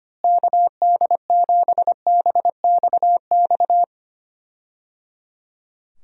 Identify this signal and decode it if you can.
Morse code, heard as a keyed tone, about 25 wpm, 700 Hz. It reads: KD7BXX
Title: USA Call Signs - 6 Characters - Repeated 3x - 25wpm